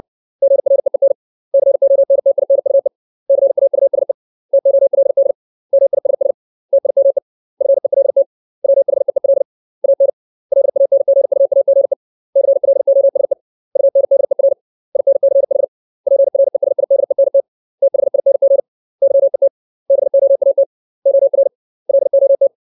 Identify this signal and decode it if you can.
Morse code, heard as a keyed tone, about 60 wpm, 550 Hz. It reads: GREEN CONTINUE CAUSE TOLD MISS TIME FELT WHEEL AN LANGUAGE CLOSE UNDER INCH PRESIDENT THING YET BOAT OR LOT
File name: 632 words / 632 Words 60WPM